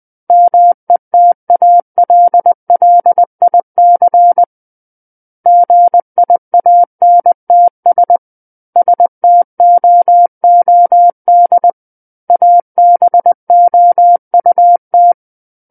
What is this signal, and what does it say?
Morse code, heard as a keyed tone, about 20 wpm, 700 Hz. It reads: METALLIC GIANTS STOOD ABOUT
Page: SKCC Learning Center - Morse Code Practice Files